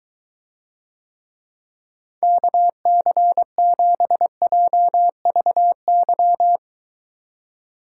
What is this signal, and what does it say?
Morse code, heard as a keyed tone, about 22 wpm, 700 Hz. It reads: KC7JVY